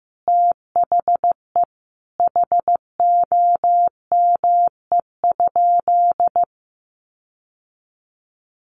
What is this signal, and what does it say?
Morse code, heard as a keyed tone, about 15 wpm, 700 Hz. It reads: THE HOME?
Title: Sentences from Top 1300 Words 15wpm